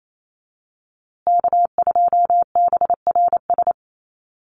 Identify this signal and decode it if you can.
Morse code, heard as a keyed tone, about 28 wpm, 700 Hz. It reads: K2BRS
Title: USA Call Signs Repeated 2x 28wpm